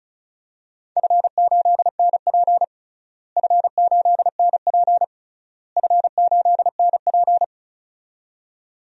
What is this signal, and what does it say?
Morse code, heard as a keyed tone, about 36 wpm, 700 Hz. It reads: F8NP F8NP F8NP
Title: DXCC-100 Call Signs - 4 Characters - Repeated 3x - 35wpm